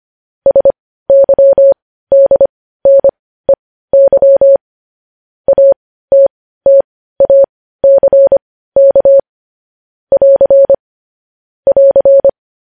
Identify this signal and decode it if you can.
Morse code, heard as a keyed tone, about 25 wpm, spaced as for 15 wpm, 550 Hz. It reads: SYDNEY ATTACK <AR> <AR>